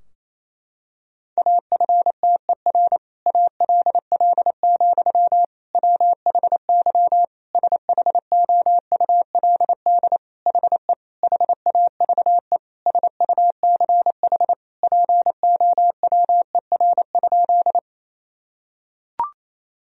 Morse code, heard as a keyed tone, about 27 wpm, 700 Hz. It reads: AFTER ALL, WHY SHOULD HE HAVE SUCH POWER?
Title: Sentences from Top 1000 Words - Review Part 2 - 28wpm